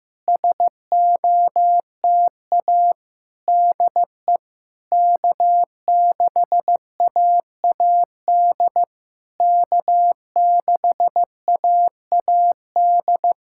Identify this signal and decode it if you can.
Morse code, heard as a keyed tone, about 15 wpm, 700 Hz. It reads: SOTA DE K6AAD K6AAD
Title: SOTA QSO Practice 15wpm